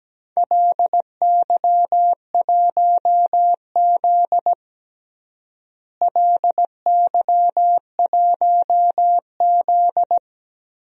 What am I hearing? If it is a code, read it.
Morse code, heard as a keyed tone, about 17 wpm, 700 Hz. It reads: LY1Z LY1Z
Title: DXCC-100 Call Signs - 4 Characters - Repeated 3x - 17wpm